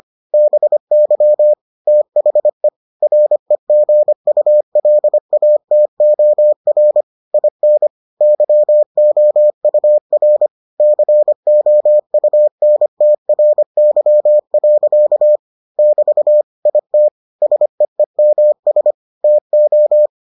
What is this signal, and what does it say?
Morse code, heard as a keyed tone, about 25 wpm, 600 Hz. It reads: BY THE REGULATOR IN YOUR COUNTRY. <BT> IT SEEMS TO